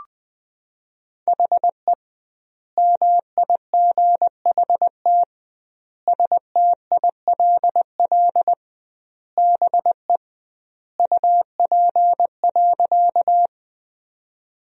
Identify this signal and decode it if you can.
Morse code, heard as a keyed tone, about 20 wpm, 700 Hz. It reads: HE MIGHT STILL BE UP.
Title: 2x Word-Spacing - Sentences from Top 300 Words - 20wpm